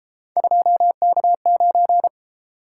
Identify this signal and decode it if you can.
Morse code, heard as a keyed tone, about 33 wpm, 700 Hz. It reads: 2K9